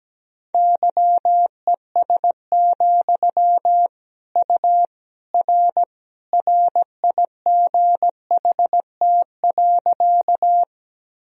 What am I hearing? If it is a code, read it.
Morse code, heard as a keyed tone, about 17 wpm, 700 Hz. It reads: YES, U R RIGHT.
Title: Sentences from Top 500 Words and Abbreviations 17wpm